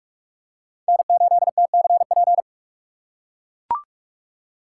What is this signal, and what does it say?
Morse code, heard as a keyed tone, about 43 wpm, 700 Hz. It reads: N8TCP
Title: USA Call Signs - 5 Characters - 3x Character Spacing - 45wpm